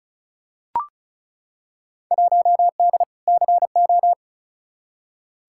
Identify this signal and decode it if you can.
Morse code, heard as a keyed tone, about 36 wpm, 700 Hz. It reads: 1D CO
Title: Field Day Exchanges - 35wpm